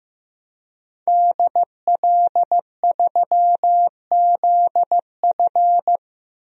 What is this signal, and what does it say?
Morse code, heard as a keyed tone, about 15 wpm, 700 Hz. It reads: DL3ZF
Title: DXCC-100 Call Signs - 5 Characters - 15wpm